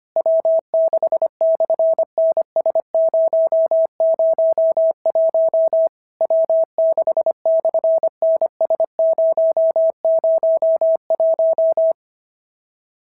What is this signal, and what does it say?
Morse code, heard as a keyed tone, about 25 wpm, 690 Hz. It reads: W6/NS001 W6/NS001
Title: SOTA QSO Practice 25wpm